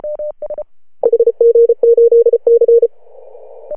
Morse code, heard as a keyed tone, about 34 wpm, 470 Hz. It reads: HG8C
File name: CW / HA2OS RHOB 2017 CW